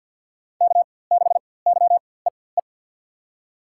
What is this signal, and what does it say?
Morse code, heard as a keyed tone, about 46 wpm, spaced as for 23 wpm, 700 Hz. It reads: K6YEE
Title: USA Call Signs - 5 Characters - 4x Character Spacing - 50wpm